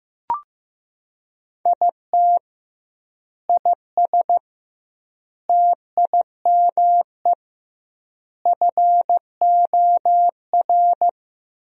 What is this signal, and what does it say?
Morse code, heard as a keyed tone, about 15 wpm, 700 Hz. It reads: IT IS TIME FOR
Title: 2x Word-Spacing - Sentences from Top 500 Words - 15wpm